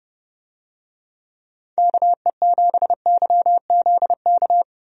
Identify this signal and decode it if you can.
Morse code, heard as a keyed tone, about 30 wpm, 700 Hz. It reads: KE7YZK
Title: USA Call Signs - 6 Characters - Repeated 2x - 30wpm